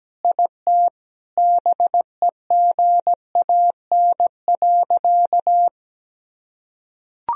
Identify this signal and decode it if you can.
Morse code, heard as a keyed tone, about 17 wpm, 700 Hz. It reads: IT BEGAN.
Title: Sentences from Top 400 Words and Abbreviations 17wpm